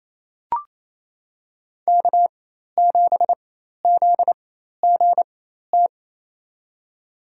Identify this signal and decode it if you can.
Morse code, heard as a keyed tone, about 28 wpm, spaced as for 13 wpm, 700 Hz. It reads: K7ZGT
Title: USA Call Signs - 5 Characters - 4x Character Spacing - 28wpm